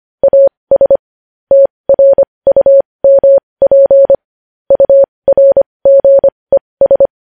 Morse code, heard as a keyed tone, about 25 wpm, spaced as for 20 wpm, 550 Hz. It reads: AS TRUMP URGES
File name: podcast / News headlines-1x-25x20